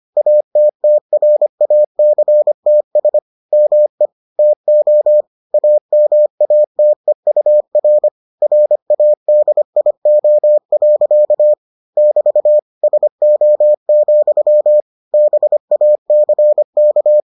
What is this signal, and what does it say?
Morse code, heard as a keyed tone, about 25 wpm, 600 Hz. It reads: ATTRACTS ME TO AMATEUR RADIO. <BT> SO, BACK